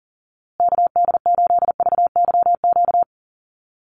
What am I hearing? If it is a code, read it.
Morse code, heard as a keyed tone, about 40 wpm, 700 Hz. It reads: KD8VYQ
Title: USA Call Signs Repeated 2x 40wpm